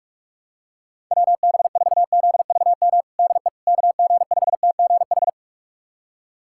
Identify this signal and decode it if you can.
Morse code, heard as a keyed tone, about 46 wpm, 700 Hz. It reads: WB4ZVM DE KG5TGH
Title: USA Callsign DE Callsign - 6 Characters - 45wpm